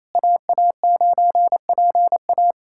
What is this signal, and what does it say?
Morse code, heard as a keyed tone, about 27 wpm, 700 Hz. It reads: AA9PA
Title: USA Callsign DE Callsign - 5 Characters - 28wpm